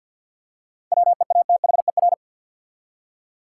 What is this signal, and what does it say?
Morse code, heard as a keyed tone, about 52 wpm, 700 Hz. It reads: WEATHER
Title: Seven Letter Words - 50wpm